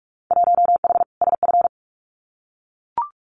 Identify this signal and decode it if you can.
Morse code, heard as a keyed tone, about 44 wpm, 700 Hz. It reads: 1H SF
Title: Winter Field Day Exchanges 45wpm